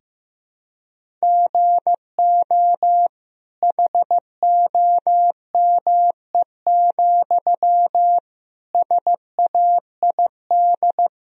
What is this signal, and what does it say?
Morse code, heard as a keyed tone, about 15 wpm, 700 Hz. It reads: GO HOME, SAID
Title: Sentences from Top 1000 Words - Review Part 1 - 15wpm